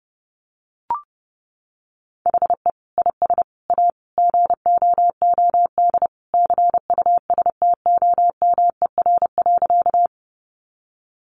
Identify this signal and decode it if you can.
Morse code, heard as a keyed tone, about 30 wpm, 700 Hz. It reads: HE IS A GOOD CUSTOMER.
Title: Sentences from Top 1100 Words 30wpm